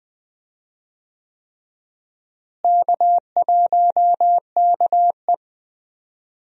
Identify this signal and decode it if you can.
Morse code, heard as a keyed tone, about 21 wpm, 700 Hz. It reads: K1KE
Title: USA Call Signs - 4 Characters - Repeated 3x - 20wpm